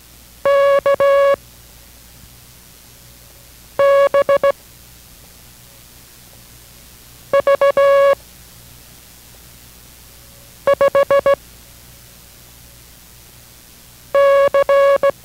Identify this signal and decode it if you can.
Morse code, heard as a keyed tone, about 17 wpm, spaced as for 3 wpm, 560 Hz. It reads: KBV5C